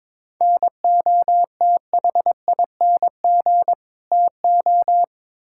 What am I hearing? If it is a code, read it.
Morse code, heard as a keyed tone, about 22 wpm, 700 Hz. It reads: NOTHING TO
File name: practice / 22wpm / Build A Phrase - 5-Word Phrases - 22wpm